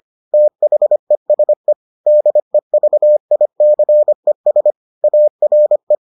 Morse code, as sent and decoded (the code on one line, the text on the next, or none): - .... ... / -.. ...- .. -.-. ... / .- .-. .
THESE DEVICES ARE